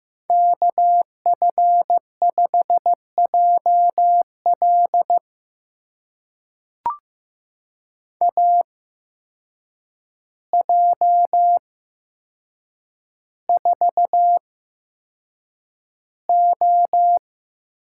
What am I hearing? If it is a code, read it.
Morse code, heard as a keyed tone, about 15 wpm, spaced as for 7 wpm, 700 Hz. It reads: KF5JL AJ4O
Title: USA Call Signs - 5 Characters - 8x Character Spacing - 15wpm